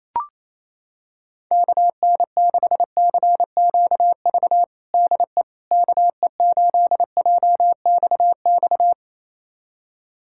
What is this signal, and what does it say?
Morse code, heard as a keyed tone, about 28 wpm, 700 Hz. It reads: KN6CQV DE KE8JXX
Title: USA Callsign DE Callsign - 6 Characters - 28wpm